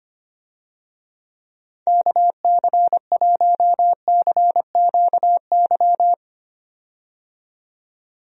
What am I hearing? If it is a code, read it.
Morse code, heard as a keyed tone, about 25 wpm, 700 Hz. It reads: KC1CQY
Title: USA Call Signs - 6 Characters - 25wpm